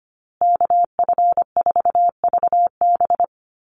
Morse code, heard as a keyed tone, about 25 wpm, 700 Hz. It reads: KF4VB
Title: USA Call Signs Repeated 2x 25wpm